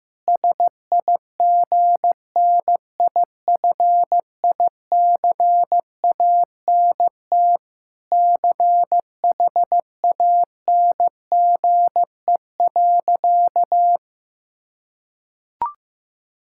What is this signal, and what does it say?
Morse code, heard as a keyed tone, about 15 wpm, 700 Hz. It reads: SIGNIFICANT CHANGE.
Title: Sentences from Top 1000 Words - Review Part 6 - 15wpm